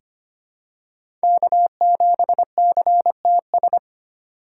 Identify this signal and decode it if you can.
Morse code, heard as a keyed tone, about 25 wpm, 700 Hz. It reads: K7CTS
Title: USA Call Signs - 5 Characters - 2x Character Spacing - 25wpm